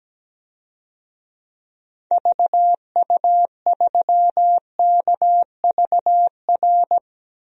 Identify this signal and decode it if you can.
Morse code, heard as a keyed tone, about 17 wpm, 700 Hz. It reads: VU3KVR
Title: DXCC-100 Call Signs - 6 Characters - Repeated 3x - 17wpm